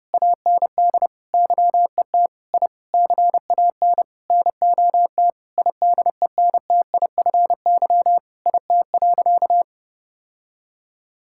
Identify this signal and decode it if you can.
Morse code, heard as a keyed tone, about 28 wpm, 700 Hz. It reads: AND YET I CAN NOT IDENTIFY IT.
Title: Sentences from Top 1000 Words - Review Part 6 - 30wpm